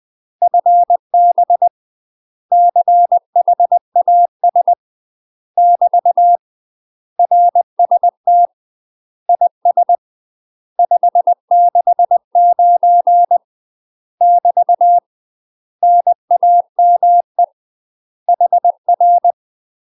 Morse code, heard as a keyed tone, about 20 wpm, 700 Hz. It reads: FB CHAS <BT> RST IS 569 <BT> NAME HR
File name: UK MASTER QSOs TRAINING / 20WPM / UK MASTER QSOs 0013